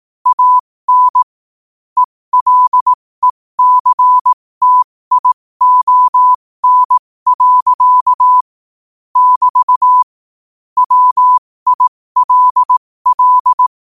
Morse code, heard as a keyed tone, about 18 wpm, spaced as for 15 wpm, 1000 Hz. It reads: AN ELECTION. <BT> WILL
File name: qotdfiles / 15wpm